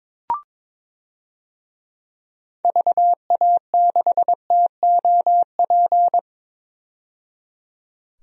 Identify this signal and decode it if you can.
Morse code, heard as a keyed tone, about 22 wpm, 700 Hz. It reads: VA6TOP
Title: DXCC-100 Call Signs - 6 Characters - 22wpm